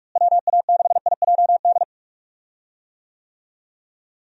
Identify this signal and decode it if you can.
Morse code, heard as a keyed tone, about 46 wpm, 700 Hz. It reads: WA6IJD
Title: USA Call Signs - 6 Characters - Repeated 3x - 45wpm